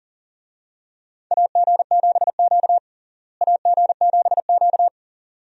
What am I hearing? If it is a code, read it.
Morse code, heard as a keyed tone, about 40 wpm, 700 Hz. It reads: AG7Q AG7Q
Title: USA Call Signs - 4 Characters - Repeated 2x - 40wpm